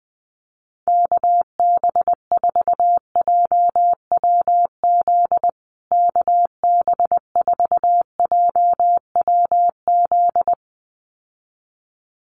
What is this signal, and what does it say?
Morse code, heard as a keyed tone, about 20 wpm, 700 Hz. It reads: KB4JWZ KB4JWZ
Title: USA Call Signs Repeated 2x 20wpm